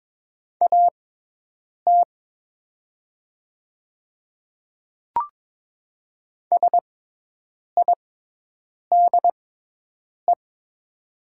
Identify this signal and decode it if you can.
Morse code, heard as a keyed tone, about 22 wpm, spaced as for 8 wpm, 700 Hz. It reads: AT SIDE
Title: 6x Character Spacing - Top 100 Words - 22wpm